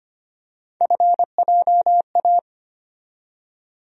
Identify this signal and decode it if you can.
Morse code, heard as a keyed tone, about 25 wpm, 700 Hz. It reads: FJA
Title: USA Call Sign Postfixes - 3 Letters 25wpm